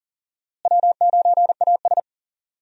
Morse code, heard as a keyed tone, about 40 wpm, 700 Hz. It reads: W9AS
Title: USA Call Signs - 4 Characters - Repeated 3x - 40wpm